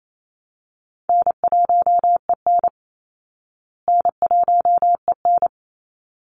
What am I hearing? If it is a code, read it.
Morse code, heard as a keyed tone, about 28 wpm, 700 Hz. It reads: N1EN N1EN
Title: Easy Call Signs Repeated 2x - 28wpm